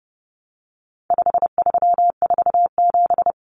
Morse code, heard as a keyed tone, about 30 wpm, 700 Hz. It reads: 5347
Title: Common Cut Numbers - 4-Digit 30wpm